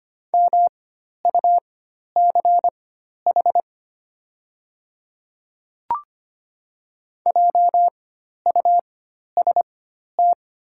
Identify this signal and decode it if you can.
Morse code, heard as a keyed tone, about 25 wpm, spaced as for 12 wpm, 690 Hz. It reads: MUCH JUST
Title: 4x Character Spacing - Top 200 Words - 25wpm